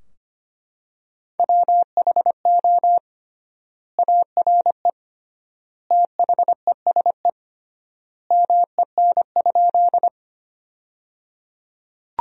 Morse code, heard as a keyed tone, about 25 wpm, 700 Hz. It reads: WHO ARE THESE MEN?
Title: 3x Word-Spacing - Sentences from Top 200 Words - 25wpm